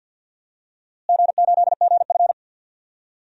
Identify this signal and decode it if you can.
Morse code, heard as a keyed tone, about 48 wpm, 700 Hz. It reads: G8GF
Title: DXCC-100 Call Signs - 4 Characters - Repeated 2x - 50wpm